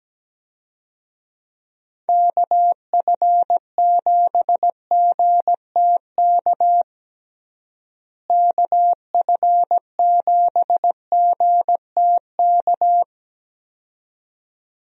Morse code, heard as a keyed tone, about 17 wpm, 700 Hz. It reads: KF7GTK KF7GTK
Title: USA Call Signs - 6 Characters - Repeated 2x - 17wpm